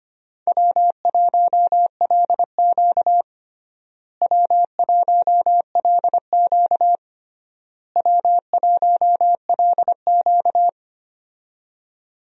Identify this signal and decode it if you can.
Morse code, heard as a keyed tone, about 25 wpm, 700 Hz. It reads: W1LQ W1LQ W1LQ
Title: USA Call Signs - 4 Characters - Repeated 3x - 25wpm